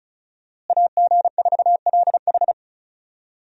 Morse code, heard as a keyed tone, about 34 wpm, 700 Hz. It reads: AG4LH
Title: USA Call Signs - 5 Characters - Repeated 3x - 35wpm